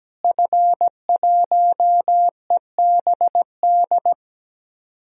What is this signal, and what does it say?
Morse code, heard as a keyed tone, about 17 wpm, 700 Hz. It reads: F1EBD
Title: DXCC-100 Call Signs - 5 Characters - Repeated 3x - 17wpm